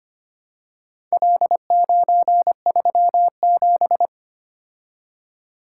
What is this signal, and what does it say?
Morse code, heard as a keyed tone, about 25 wpm, 700 Hz. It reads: L937
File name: practice / 25wpm / Four Letter-Number Combinations - 25wpm